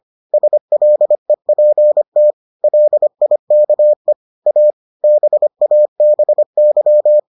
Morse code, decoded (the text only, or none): SLEPT LIKE A BABY